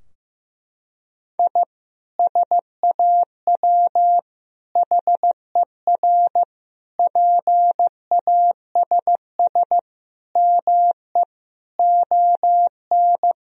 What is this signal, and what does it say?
Morse code, heard as a keyed tone, about 15 wpm, 700 Hz. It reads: I SAW HER PASS ME ON
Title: Sentences from Top 1000 Words - Review Part 2 - 15wpm